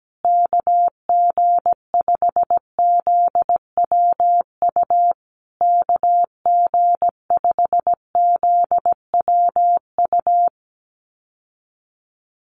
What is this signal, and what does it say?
Morse code, heard as a keyed tone, about 17 wpm, 700 Hz. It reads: KG5ZWU KG5ZWU
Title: USA Call Signs Repeated 3x 17wpm